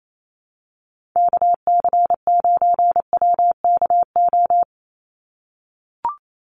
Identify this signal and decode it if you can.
Morse code, heard as a keyed tone, about 28 wpm, 700 Hz. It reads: KC9WKO
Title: USA Call Signs 28wpm